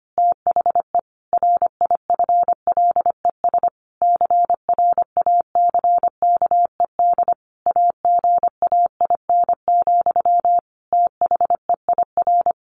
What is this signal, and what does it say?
Morse code, heard as a keyed tone, about 25 wpm, 700 Hz. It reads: THE RIFLES CRACKED AGAIN, THEIR
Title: A Sound Of Thunder 25wpm